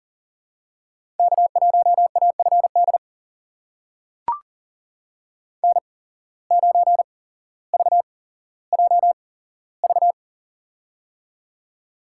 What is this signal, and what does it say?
Morse code, heard as a keyed tone, about 40 wpm, spaced as for 17 wpm, 700 Hz. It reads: K1AFD N9VJV
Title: USA Call Signs - 5 Characters - 8x Character Spacing - 40wpm